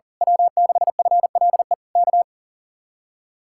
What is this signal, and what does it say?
Morse code, heard as a keyed tone, about 38 wpm, 700 Hz. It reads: W6FLE K
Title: Speed-Racing Common QSO Phrases Encore - 40wpm